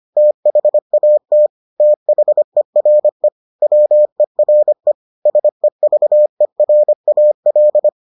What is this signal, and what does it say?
Morse code, heard as a keyed tone, about 25 wpm, 600 Hz. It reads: THAT THERE WERE SEVERAL